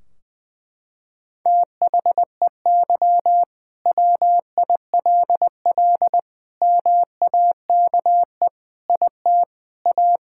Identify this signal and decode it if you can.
Morse code, heard as a keyed tone, about 20 wpm, 700 Hz. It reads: THEY WILL MAKE IT A
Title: Sentences from Top 1600 Words 20wpm